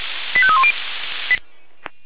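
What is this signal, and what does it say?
Five-tone selective call (ZVEI), heard as quick runs of five tones, tones of about 70 ms. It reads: 85310
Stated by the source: ZVEI five-tone fire brigade alert received with an SDR by Feuerwehr Orlamünde.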